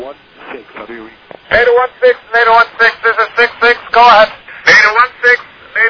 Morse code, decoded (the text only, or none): none